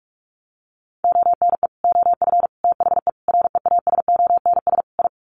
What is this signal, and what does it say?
Morse code, heard as a keyed tone, about 44 wpm, 700 Hz. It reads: ONE OF THE REASONS I
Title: Sets of 5 Words Encore2 45wpm